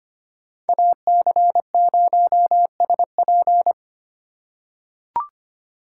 Morse code, heard as a keyed tone, about 25 wpm, 700 Hz. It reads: AC0SP
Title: USA Call Signs - 5 Characters - 2x Character Spacing - 25wpm